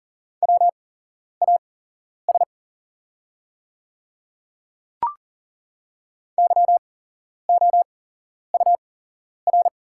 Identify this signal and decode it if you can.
Morse code, heard as a keyed tone, about 40 wpm, spaced as for 11 wpm, 700 Hz. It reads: WAS YOUR
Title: 8x Character Spacing - Top 100 Words - 40wpm